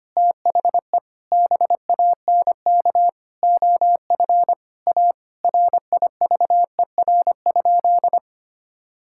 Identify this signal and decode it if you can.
Morse code, heard as a keyed tone, about 25 wpm, 700 Hz. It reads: THE BANK OF A RIVER?
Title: Sentences from Top 1000 Words - Review Part 6 - 25wpm